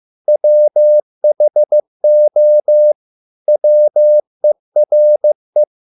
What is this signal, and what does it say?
Morse code, heard as a keyed tone, about 15 wpm, 600 Hz. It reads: WHO WERE